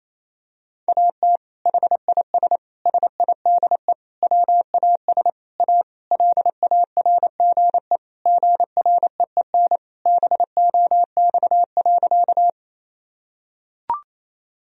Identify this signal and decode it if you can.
Morse code, heard as a keyed tone, about 28 wpm, 700 Hz. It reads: AT HIS SIDE WAS A LARGE GREEN BOX.
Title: Sentences from Top 1000 Words - Review Part 2 - 28wpm